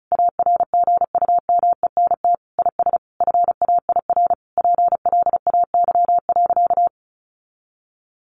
Morse code, heard as a keyed tone, about 34 wpm, 700 Hz. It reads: ARGUMENT IS FAIR PLAY.
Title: Sentences from Top 1400 Words 35wpm